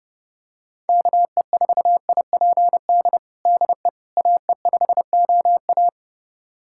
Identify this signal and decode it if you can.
Morse code, heard as a keyed tone, about 30 wpm, 700 Hz. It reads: KE4IPD DE AE5OA
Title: USA Callsign DE Callsign 30wpm